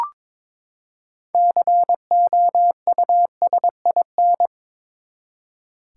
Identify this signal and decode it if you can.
Morse code, heard as a keyed tone, about 22 wpm, 700 Hz. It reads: COUSIN
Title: Six Letter Words - 22wpm